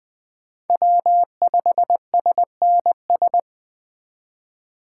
Morse code, heard as a keyed tone, about 21 wpm, 700 Hz. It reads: W5SNS
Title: USA Call Signs - 5 Characters - 3x Character Spacing - 20wpm